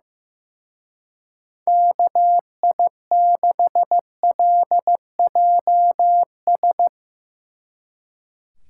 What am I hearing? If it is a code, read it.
Morse code, heard as a keyed tone, about 16 wpm, 700 Hz. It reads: KI6LJS
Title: Speed Racing - USA 2x3 Call Signs 15wpm